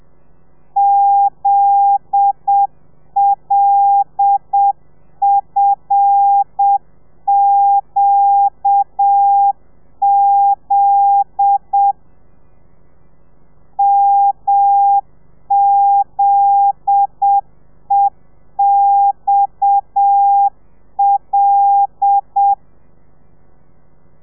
Morse code, heard as a keyed tone, about 7 wpm, 800 Hz. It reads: ZLFQZ MZEXL